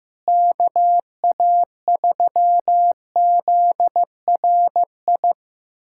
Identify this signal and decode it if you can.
Morse code, heard as a keyed tone, about 15 wpm, 700 Hz. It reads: KA3ZRI
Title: USA Callsign DE Callsign - 6 Characters - 15wpm